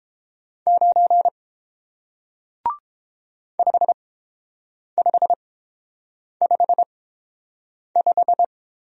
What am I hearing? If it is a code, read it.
Morse code, heard as a keyed tone, about 30 wpm, spaced as for 7 wpm, 700 Hz. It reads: 9 5555